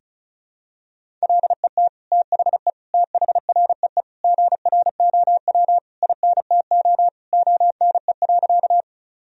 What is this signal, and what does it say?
Morse code, heard as a keyed tone, about 36 wpm, 700 Hz. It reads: LET THE THREE GROW INTO ONE.
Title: Sentences from Top 1000 Words - Review Part 1 - 35wpm